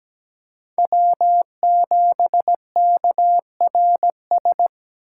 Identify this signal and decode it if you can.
Morse code, heard as a keyed tone, about 17 wpm, 700 Hz. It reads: W7KRS
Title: K1USN SST QSO Practice - 17wpm